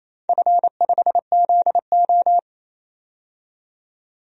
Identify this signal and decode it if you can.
Morse code, heard as a keyed tone, about 28 wpm, 700 Hz. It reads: F5ZO
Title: DXCC-100 Call Signs - 4 Characters - 28wpm